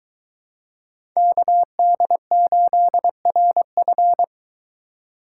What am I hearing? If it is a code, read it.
Morse code, heard as a keyed tone, about 21 wpm, 700 Hz. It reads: KD8RF